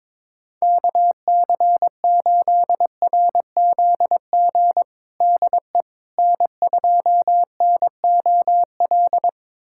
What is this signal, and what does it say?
Morse code, heard as a keyed tone, about 22 wpm, 700 Hz. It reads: KC8RZG DE N2NOL
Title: USA Callsign DE Callsign 22wpm